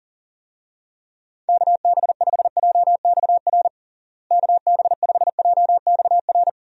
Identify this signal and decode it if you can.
Morse code, heard as a keyed tone, about 40 wpm, 700 Hz. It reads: KB5JXR KB5JXR
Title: USA Call Signs - 6 Characters - Repeated 3x - 40wpm